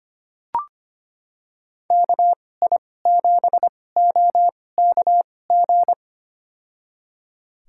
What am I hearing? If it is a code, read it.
Morse code, heard as a keyed tone, about 25 wpm, spaced as for 18 wpm, 700 Hz. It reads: KI7OKG
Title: USA Call Signs - 2x Character Spacing - 25wpm